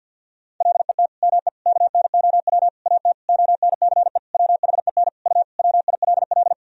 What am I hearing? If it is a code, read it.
Morse code, heard as a keyed tone, about 50 wpm, 700 Hz. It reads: LET ME KNOW AT ONCE WHEN U WILL
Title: Sentences from Top 300 Words and Abbreviations 50wpm